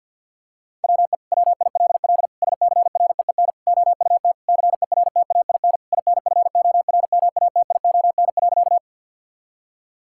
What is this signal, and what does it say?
Morse code, heard as a keyed tone, about 50 wpm, 700 Hz. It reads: WE WILL SCREEN OUT CERTAIN INFORMATION.
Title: Sentences from Top 1000 Words - Review Part 6 - 50wpm